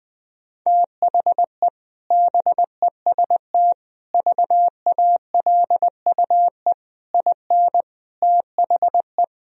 Morse code, heard as a keyed tone, about 20 wpm, 700 Hz. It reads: THE BEST VALUE IN THE
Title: Sentences from Top 1000 Words - Review Part 3 - 20wpm